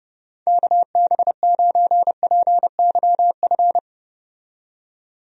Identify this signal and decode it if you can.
Morse code, heard as a keyed tone, about 30 wpm, 700 Hz. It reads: KB9PYF
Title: USA Call Signs - 6 Characters - 30wpm